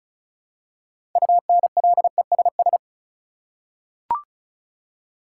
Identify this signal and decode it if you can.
Morse code, heard as a keyed tone, about 36 wpm, 700 Hz. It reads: UNLESS
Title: Six Letter Words - 35wpm